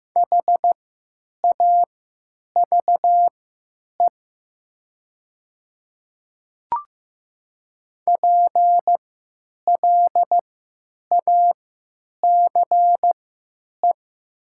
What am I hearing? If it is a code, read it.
Morse code, heard as a keyed tone, about 15 wpm, spaced as for 8 wpm, 700 Hz. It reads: HAVE PLACE
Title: 3x Character Spacing - Top 200 Words - 15wpm